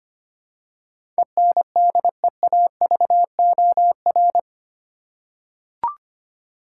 Morse code, heard as a keyed tone, about 25 wpm, 700 Hz. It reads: ENDEAVOR
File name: 8 Letter Words - First 3 Letters Given - 25wpm